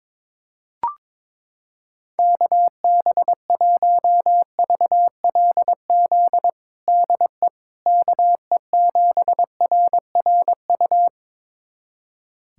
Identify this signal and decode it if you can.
Morse code, heard as a keyed tone, about 22 wpm, 700 Hz. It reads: KB1VLZ DE KE7RRU
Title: USA Callsign DE Callsign 22wpm